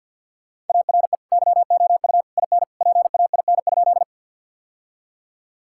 Morse code, heard as a keyed tone, about 50 wpm, 700 Hz. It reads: ARE YOU IN PAIN?
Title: Sentences from Top 1600 Words 50wpm